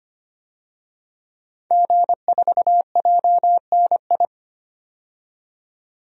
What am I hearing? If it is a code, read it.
Morse code, heard as a keyed tone, about 25 wpm, 700 Hz. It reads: G4JNI
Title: DXCC-100 Call Signs - 5 Characters - 25wpm